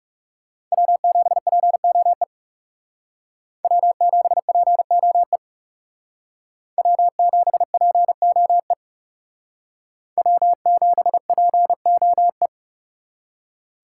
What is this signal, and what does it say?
Morse code, heard as a keyed tone, about 36 wpm, 700 Hz. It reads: W7POE W7POE W7POE W7POE
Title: Speed Racing - USA Call Signs 30wpm